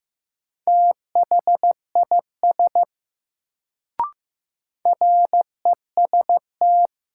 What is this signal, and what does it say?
Morse code, heard as a keyed tone, about 15 wpm, 700 Hz. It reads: THIS REST
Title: Sets of 2 Words - New Word Drill - 15wpm